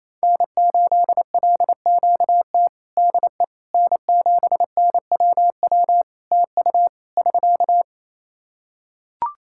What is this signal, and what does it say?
Morse code, heard as a keyed tone, about 28 wpm, 700 Hz. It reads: N8LQT DE N7NWW TU <SK>
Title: Practice QSO - Exchange 4 - Ending - 28wpm